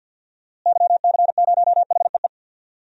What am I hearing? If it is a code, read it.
Morse code, heard as a keyed tone, about 53 wpm, 700 Hz. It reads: YC0HEE
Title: DXCC-100 Call Signs - 6 Characters - Repeated 2x - 50wpm